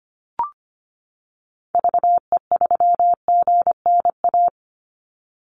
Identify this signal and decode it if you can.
Morse code, heard as a keyed tone, about 25 wpm, 700 Hz. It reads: VE3GNA
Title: Call Signs - 25wpm